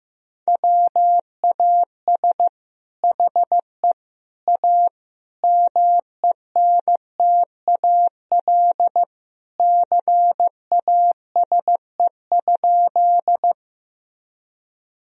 Morse code, heard as a keyed tone, about 15 wpm, 700 Hz. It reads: WAS HE A MENTAL CASE?
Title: Sentences from Top 1000 Words - Review Part 4 - 15wpm